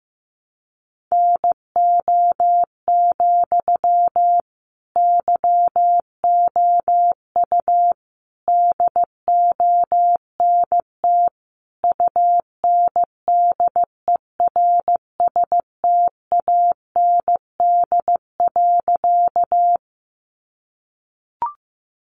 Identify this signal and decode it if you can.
Morse code, heard as a keyed tone, about 15 wpm, 700 Hz. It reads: NO, YOU DONT UNDERSTAND.
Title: Sentences from Top 600 Words 15wpm